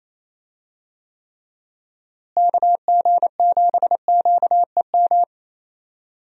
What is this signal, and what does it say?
Morse code, heard as a keyed tone, about 28 wpm, 700 Hz. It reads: KG7QEM
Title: USA Call Signs - 6 Characters - Repeated 3x - 28wpm